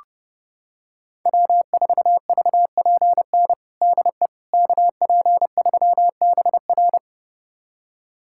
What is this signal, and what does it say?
Morse code, heard as a keyed tone, about 30 wpm, 700 Hz. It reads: W4VPN DE KP3BR
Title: USA Callsign DE Callsign - 5 Characters - 30wpm